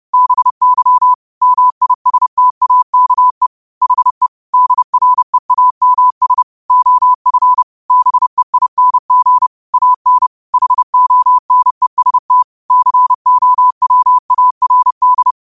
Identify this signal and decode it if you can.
Morse code, heard as a keyed tone, about 30 wpm, 1000 Hz. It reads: BY MISTAKE HE DREAMS OF BEING AN HONEST COWARD